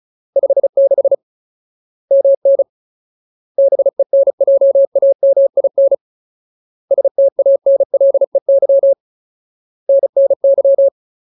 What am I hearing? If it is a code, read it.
Morse code, heard as a keyed tone, about 34 wpm, 550 Hz. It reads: 56 MN BENJAMIN STANLEY NNY
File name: Ultimate Word List / 35WPM Ultimate